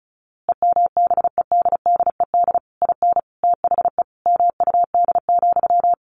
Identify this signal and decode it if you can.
Morse code, heard as a keyed tone, about 34 wpm, 700 Hz. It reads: EMBEDDED IN THE MUD,
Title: A Sound Of Thunder 35wpm